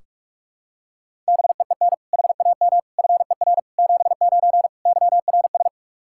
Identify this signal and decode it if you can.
Morse code, heard as a keyed tone, about 45 wpm, 700 Hz. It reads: BEEN HAM FER 79 YRS
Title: Practice QSO - Exchange 3 - More Info - 45wpm